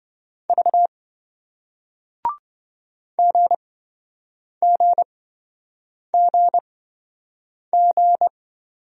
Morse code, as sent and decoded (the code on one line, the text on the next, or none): ...- / --. --. --. --.
V GGGG